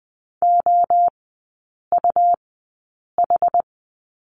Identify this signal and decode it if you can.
Morse code, heard as a keyed tone, about 20 wpm, spaced as for 8 wpm, 700 Hz. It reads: OUH
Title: 100 Lesson 17e - Letter u - Three Characters 20wpm